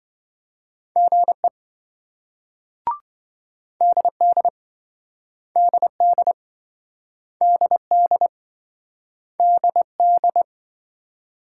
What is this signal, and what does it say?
Morse code, heard as a keyed tone, about 27 wpm, 700 Hz. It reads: GE DD DD DD DD